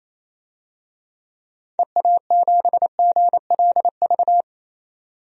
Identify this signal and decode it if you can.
Morse code, heard as a keyed tone, about 28 wpm, 700 Hz. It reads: EA7GLV
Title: DXCC-100 Call Signs - 6 Characters - 28wpm